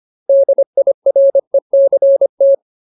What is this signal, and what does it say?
Morse code, heard as a keyed tone, about 25 wpm, 550 Hz. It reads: DIRECT
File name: Ultimate Word List / 25WPM Ultimate